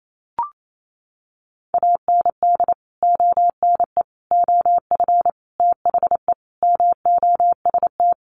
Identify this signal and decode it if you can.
Morse code, heard as a keyed tone, about 28 wpm, 700 Hz. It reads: AND ONE OF THE MOST
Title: Sets of 5 Words Encore 28wpm